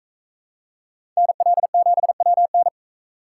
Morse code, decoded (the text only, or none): NL7WN